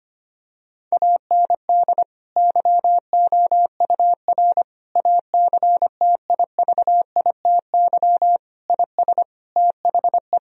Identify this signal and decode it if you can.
Morse code, heard as a keyed tone, about 25 wpm, 700 Hz. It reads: AND YOUR ACTIVITY IS THE
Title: Sentences from Top 1000 Words - Review Part 4 - 25wpm